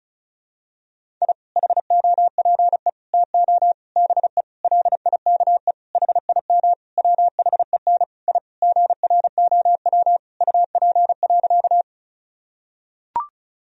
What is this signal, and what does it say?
Morse code, heard as a keyed tone, about 34 wpm, 700 Hz. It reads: I HOPE TO BE LIKE HIM WHEN I GROW UP.
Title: Sentences from Top 1000 Words - Review Part 1 - 35wpm